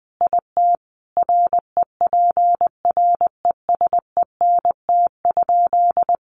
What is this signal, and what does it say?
Morse code, heard as a keyed tone, about 20 wpm, 700 Hz. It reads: IT REPRESENT?
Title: Sentences from Top 1200 Words 20wpm